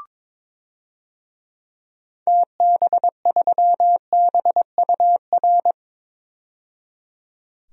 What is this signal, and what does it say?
Morse code, heard as a keyed tone, about 22 wpm, 700 Hz. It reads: TB3BUR
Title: DXCC-100 Call Signs - 6 Characters - 22wpm